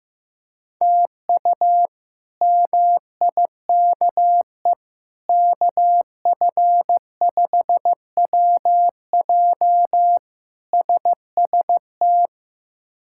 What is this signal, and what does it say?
Morse code, heard as a keyed tone, about 15 wpm, 700 Hz. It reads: TU MIKE KF5WJ SST
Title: K1USN SST QSO Practice - 15wpm